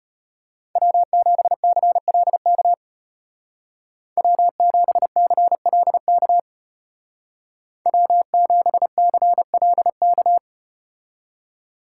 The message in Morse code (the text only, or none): W7CLK W7CLK W7CLK